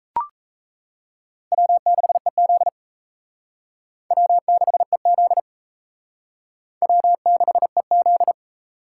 Morse code, heard as a keyed tone, about 35 wpm, 700 Hz. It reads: W6EZ W6EZ W6EZ
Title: Speed Racing - USA 1x2 Call Signs 28wpm